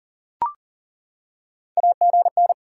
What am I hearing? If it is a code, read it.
Morse code, heard as a keyed tone, about 40 wpm, 700 Hz. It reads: AGN
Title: Top 90 QSO Elements 40wpm